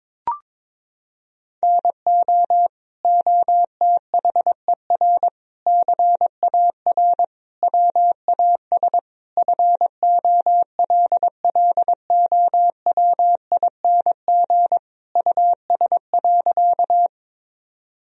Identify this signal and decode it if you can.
Morse code, heard as a keyed tone, about 21 wpm, 700 Hz. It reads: NO OTHER CAR WAS FOLLOWING US.
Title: Sentences from Top 1000 Words - Review Part 4 - 22wpm